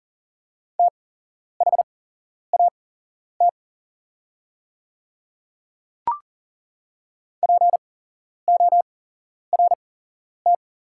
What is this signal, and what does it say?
Morse code, heard as a keyed tone, about 40 wpm, spaced as for 11 wpm, 690 Hz. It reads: THAT PORT
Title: 8x Character Spacing - Top 200 Words - 40wpm